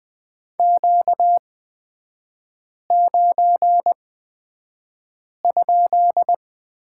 Morse code, heard as a keyed tone, about 20 wpm, spaced as for 5 wpm, 700 Hz. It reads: Q9?
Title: 250 Lesson 40e - BK - Three Characters 20wpm